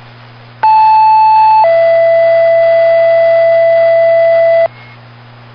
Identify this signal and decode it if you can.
Two-tone page, heard as a one-second tone, then a three-second tone, about 870 then 670 Hz, with noise. It is heard from 0.6 to 4.7 s.